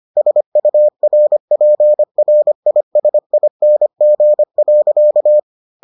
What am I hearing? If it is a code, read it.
Morse code, heard as a keyed tone, about 25 wpm, 600 Hz. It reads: SURPRISING.